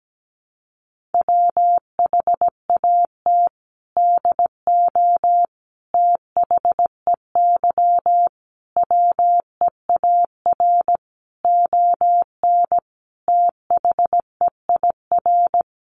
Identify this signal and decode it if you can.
Morse code, heard as a keyed tone, about 17 wpm, 700 Hz. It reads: WHAT DO THEY WEAR ON THEIR
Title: Sentences from Top 1500 Words 17wpm